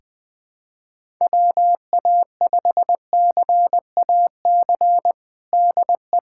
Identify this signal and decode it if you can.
Morse code, heard as a keyed tone, about 21 wpm, 700 Hz. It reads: WA5CAC DE
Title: USA Callsign DE Callsign 20wpm